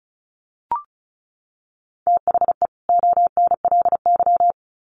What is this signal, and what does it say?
Morse code, heard as a keyed tone, about 34 wpm, 700 Hz. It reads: THE ONLY
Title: Sets of 2 Words 35wpm